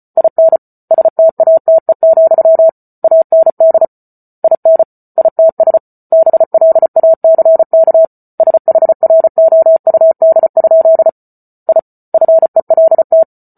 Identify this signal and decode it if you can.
Morse code, heard as a keyed tone, about 34 wpm, 650 Hz. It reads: IN STATE, AND IN ITS BLACK SHROUD? I FELT